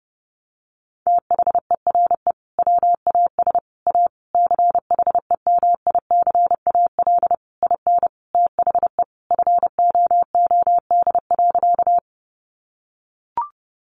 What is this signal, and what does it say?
Morse code, heard as a keyed tone, about 30 wpm, 700 Hz. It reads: THERE WAS A CHEMICAL IN THE FOOD.
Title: Sentences from Top 1200 Words 30wpm